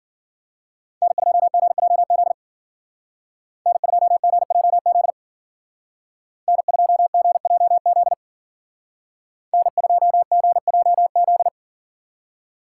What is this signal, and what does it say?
Morse code, heard as a keyed tone, about 49 wpm, 700 Hz. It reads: N2GJZ N2GJZ N2GJZ N2GJZ
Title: Speed Racing - USA 1x3 Call Signs 40wpm